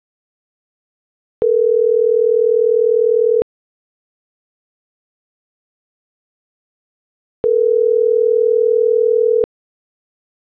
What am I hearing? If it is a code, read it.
Ringback tone (North America), 440 + 480 Hz.